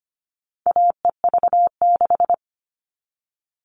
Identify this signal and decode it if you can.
Morse code, heard as a keyed tone, about 25 wpm, 700 Hz. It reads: AEV6
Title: All Cut Numbers - 4-Digit 25wpm